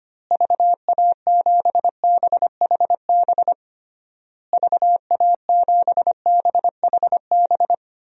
Morse code, heard as a keyed tone, about 25 wpm, 700 Hz. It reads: VA7BHB VA7BHB
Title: DXCC-100 Call Signs - 6 Characters - Repeated 3x - 25wpm